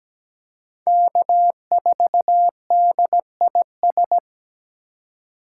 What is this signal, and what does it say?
Morse code, heard as a keyed tone, about 17 wpm, 700 Hz. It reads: K4DIS
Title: USA Call Signs - 5 Characters - 4x Character Spacing - 17wpm